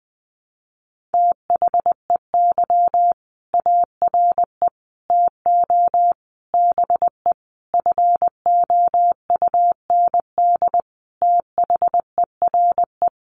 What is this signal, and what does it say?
Morse code, heard as a keyed tone, about 20 wpm, 700 Hz. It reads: THEY ARE TO BE FOUND THERE
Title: Sentences from Top 1400 Words 20wpm